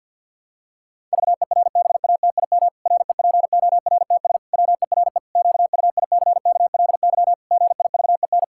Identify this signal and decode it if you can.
Morse code, heard as a keyed tone, about 51 wpm, 700 Hz. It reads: VERBATIM REPORTS WERE QUICKLY GIVEN
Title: Pangram Challenge v2 - 50wpm